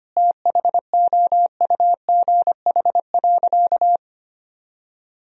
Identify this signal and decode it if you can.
Morse code, heard as a keyed tone, about 25 wpm, 700 Hz. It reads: THOUGH.
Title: Sentences from Top 1000 Words - Review Part 2 - 25wpm